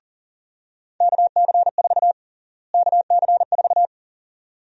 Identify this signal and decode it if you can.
Morse code, heard as a keyed tone, about 40 wpm, 700 Hz. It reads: KC4 KC4
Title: USA Call Sign Prefixes - Repeated 2x - 40wpm